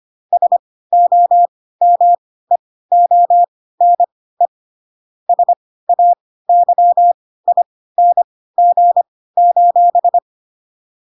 Morse code, heard as a keyed tone, about 25 wpm, spaced as for 16 wpm, 700 Hz. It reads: SOMEONE SAYING:
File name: Pinocchio / Pinocchio ch4 10 25 ik1hge